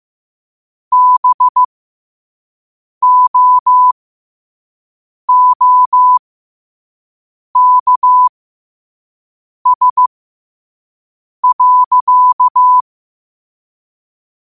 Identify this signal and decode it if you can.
Morse code, heard as a keyed tone, about 15 wpm, spaced as for 5 wpm, 1000 Hz. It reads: BOOKS.
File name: qotdfiles / 5wpm